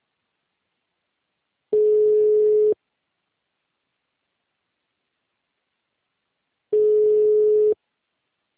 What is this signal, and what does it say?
Ringback tone (Europe, 425 Hz).
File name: out- -2003- - - .70437